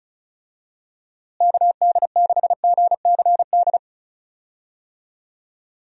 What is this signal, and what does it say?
Morse code, heard as a keyed tone, about 36 wpm, 700 Hz. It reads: KD6GCD
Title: USA Call Signs - 6 Characters - Repeated 2x - 35wpm